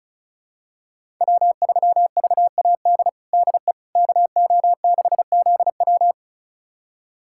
Morse code, heard as a keyed tone, about 36 wpm, 700 Hz. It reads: W3VAD DE KO6ZW
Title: USA Callsign DE Callsign - 5 Characters - 35wpm